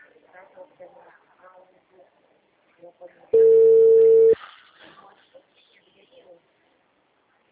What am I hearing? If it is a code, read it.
Ringback tone (China, 450 Hz), with speech.
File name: out- -2012- - - .39494